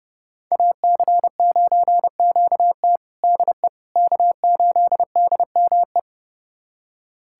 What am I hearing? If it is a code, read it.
Morse code, heard as a keyed tone, about 31 wpm, 700 Hz. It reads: AC9QT DE K8DME
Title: USA Callsign DE Callsign - 5 Characters - 30wpm